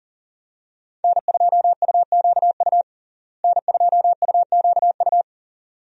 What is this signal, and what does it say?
Morse code, heard as a keyed tone, about 40 wpm, 700 Hz. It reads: N2UQU N2UQU
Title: USA Call Signs - 5 Characters - Repeated 2x - 40wpm